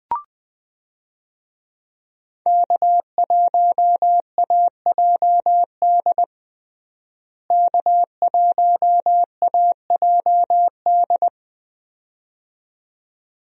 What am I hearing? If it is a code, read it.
Morse code, heard as a keyed tone, about 20 wpm, 700 Hz. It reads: K1AJD K1AJD
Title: USA Call Signs - 5 Characters - Repeated 2x - 20wpm